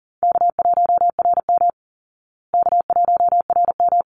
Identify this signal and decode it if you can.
Morse code, heard as a keyed tone, about 40 wpm, 700 Hz. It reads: K1RM K1RM
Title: Easy Call Signs Repeated 2x - 40wpm